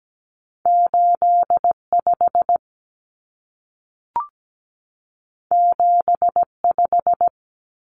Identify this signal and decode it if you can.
Morse code, heard as a keyed tone, about 17 wpm, 700 Hz. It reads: 85 75
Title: Common Cut Numbers - 2-Digit 17wpm